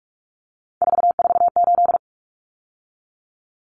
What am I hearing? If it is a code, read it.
Morse code, heard as a keyed tone, about 44 wpm, 700 Hz. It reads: 448
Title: Numbers - 3-Digit 45wpm